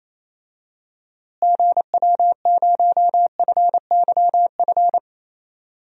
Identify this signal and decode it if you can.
Morse code, heard as a keyed tone, about 28 wpm, 700 Hz. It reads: GW0FYF
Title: DXCC-100 Call Signs - 6 Characters - Repeated 3x - 28wpm